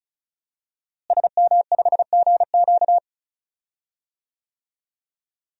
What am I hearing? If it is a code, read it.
Morse code, heard as a keyed tone, about 36 wpm, 700 Hz. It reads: SM5GQ
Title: DXCC-100 Call Signs - 5 Characters - Repeated 3x - 35wpm